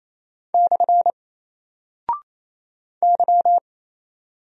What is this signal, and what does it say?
Morse code, heard as a keyed tone, about 27 wpm, 700 Hz. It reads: / Y
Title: Single Character (Letter:Number:Punctuation:Prosign) Alt 28wpm